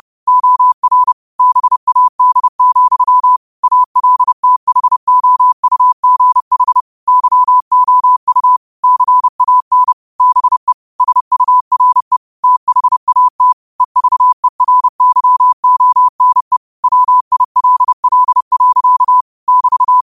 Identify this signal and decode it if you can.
Morse code, heard as a keyed tone, about 30 wpm, 1000 Hz. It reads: OR BAD, ALTHOUGH YOU CAN BE SURE THAT EVERYONE WILL. <BT>
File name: qotdfiles / 30wpm